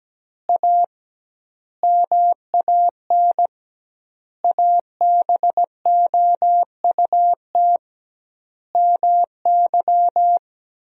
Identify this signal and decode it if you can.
Morse code, heard as a keyed tone, about 17 wpm, 700 Hz. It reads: A MAN ABOUT MY
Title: 2x Word-Spacing - Sentences from Top 400 Words - 17wpm